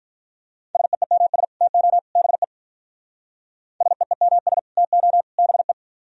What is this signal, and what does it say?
Morse code, heard as a keyed tone, about 43 wpm, 700 Hz. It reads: SEEMS TO BE SEEMS TO BE
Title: Speed-Racing - Sets of 3 Words 35wpm